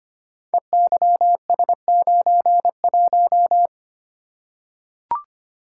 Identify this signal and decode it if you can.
Morse code, heard as a keyed tone, about 25 wpm, 700 Hz. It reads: EYS91
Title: Code Groups - 5 Letter:Numbers - 25wpm